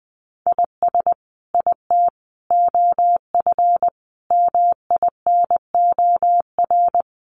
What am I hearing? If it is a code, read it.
Morse code, heard as a keyed tone, about 19 wpm, 700 Hz. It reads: IS IT OF MINOR
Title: Sentences from Top 1500 Words 20wpm